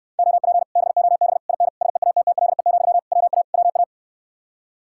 Morse code, heard as a keyed tone, about 69 wpm, 700 Hz. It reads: OP BOB IN SEATTLE, WA WA